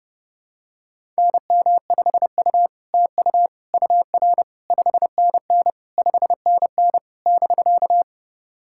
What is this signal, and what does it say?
Morse code, heard as a keyed tone, about 30 wpm, 700 Hz. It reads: NM5U TU UR 5NN 5NN <BK>
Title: SOTA QSO Practice 30wpm